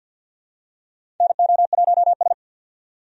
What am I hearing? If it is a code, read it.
Morse code, heard as a keyed tone, about 47 wpm, 700 Hz. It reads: NO1S
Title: USA Call Signs - 4 Characters - 50wpm